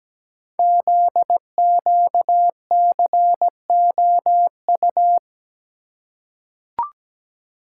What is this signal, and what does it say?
Morse code, heard as a keyed tone, about 17 wpm, 700 Hz. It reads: ZQCOU
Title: Code Groups - 5 Letters - 17wpm